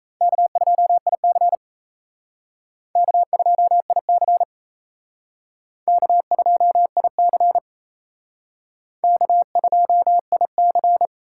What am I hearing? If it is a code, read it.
Morse code, heard as a keyed tone, about 35 wpm, 700 Hz. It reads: K2IC K2IC K2IC K2IC
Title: Speed Racing - USA Call Signs 28wpm